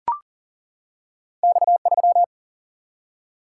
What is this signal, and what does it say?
Morse code, heard as a keyed tone, about 40 wpm, 700 Hz. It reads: X3
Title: Single-Letter Single-Number 40wpm